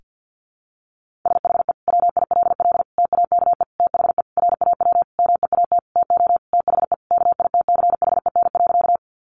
Morse code, heard as a keyed tone, about 50 wpm, 700 Hz. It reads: SHE WILL TAKE THE RAW MEAT TO THE KITCHEN.
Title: Sentences from Top 1500 Words 50wpm